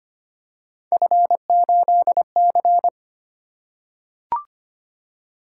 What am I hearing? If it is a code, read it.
Morse code, heard as a keyed tone, about 25 wpm, 700 Hz. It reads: F8C
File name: Three Letter-Number Combinations - 25wpm